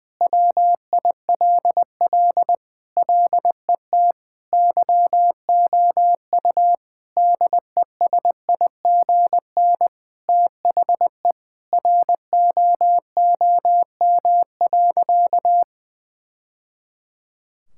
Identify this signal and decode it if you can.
Morse code, heard as a keyed tone, about 20 wpm, 700 Hz. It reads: WILL LET YOU DESIGN THE ROOM.
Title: Sentences from Top 1000 Words - Review Part 5 - 20wpm